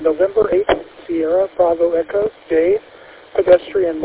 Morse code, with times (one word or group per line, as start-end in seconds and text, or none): none